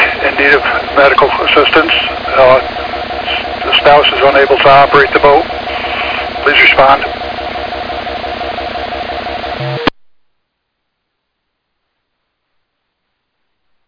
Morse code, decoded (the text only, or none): none